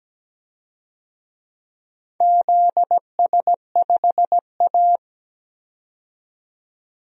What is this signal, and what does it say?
Morse code, heard as a keyed tone, about 17 wpm, 700 Hz. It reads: ZS5A
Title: DXCC-100 Call Signs - 4 Characters - 17wpm